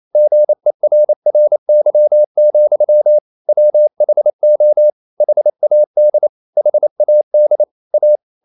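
Morse code, heard as a keyed tone, about 28 wpm, 600 Hz. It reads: GERRY, WHO HAD HAD A
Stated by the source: Morse code practice at 28 wpm, text of Ken Follett's novel Never in English, from CW med Gnister.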